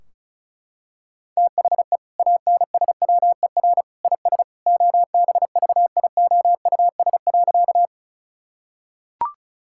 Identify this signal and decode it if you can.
Morse code, heard as a keyed tone, about 34 wpm, 700 Hz. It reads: THE ANSWER IS OBVIOUS.
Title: Sentences from Top 1600 Words 35wpm